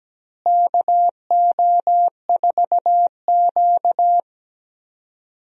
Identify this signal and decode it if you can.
Morse code, heard as a keyed tone, about 17 wpm, 700 Hz. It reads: KO4Q
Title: USA Call Signs - 4 Characters - Repeated 3x - 17wpm